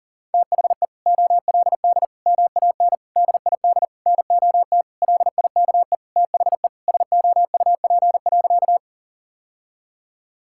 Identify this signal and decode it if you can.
Morse code, heard as a keyed tone, about 37 wpm, 700 Hz. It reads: THE OLD MAN DID NOT LIKE THE SOUP.
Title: Sentences from Top 1000 Words - Review Part 6 - 40wpm